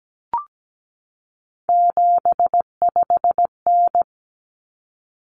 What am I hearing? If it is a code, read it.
Morse code, heard as a keyed tone, about 17 wpm, 700 Hz. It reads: 75N
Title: Common Cut Numbers - 3-Digit 17wpm